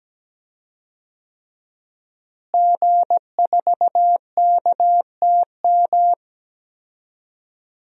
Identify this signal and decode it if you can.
Morse code, heard as a keyed tone, about 17 wpm, 700 Hz. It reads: G4KTM
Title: DXCC-100 Call Signs - 5 Characters - Repeated 3x - 17wpm